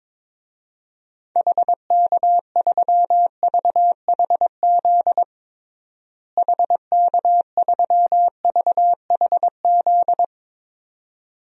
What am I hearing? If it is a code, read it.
Morse code, heard as a keyed tone, about 22 wpm, 700 Hz. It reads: HK3VHZ HK3VHZ
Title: DXCC-100 Call Signs - 6 Characters - Repeated 2x - 22wpm